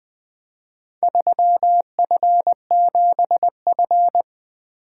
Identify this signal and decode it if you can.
Morse code, heard as a keyed tone, about 20 wpm, 700 Hz. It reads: 3F7F
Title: Speed-Racing - Four Letter-Number Combinations - 20wpm